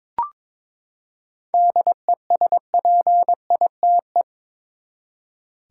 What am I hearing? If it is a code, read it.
Morse code, heard as a keyed tone, about 22 wpm, 700 Hz. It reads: DESPITE
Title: Seven Letter Words - 22wpm